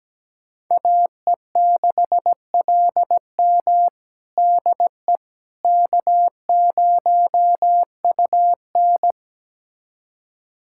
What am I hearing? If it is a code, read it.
Morse code, heard as a keyed tone, about 17 wpm, 700 Hz. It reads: AE6LM DE K0UN
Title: USA Callsign DE Callsign 17wpm